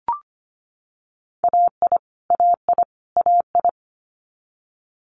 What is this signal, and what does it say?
Morse code, heard as a keyed tone, about 25 wpm, 700 Hz. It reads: AI AI AI
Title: Multi-letter Phonemes Repeated 3x 25wpm